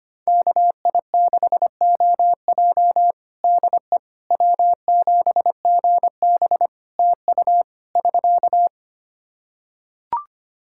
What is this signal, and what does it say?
Morse code, heard as a keyed tone, about 25 wpm, 700 Hz. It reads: KI6OJ DE W7GB TU <SK>
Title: Practice QSO - Exchange 4 - Ending - 25wpm